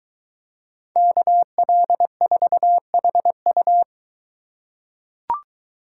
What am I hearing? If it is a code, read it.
Morse code, heard as a keyed tone, about 22 wpm, 700 Hz. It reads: KL4HU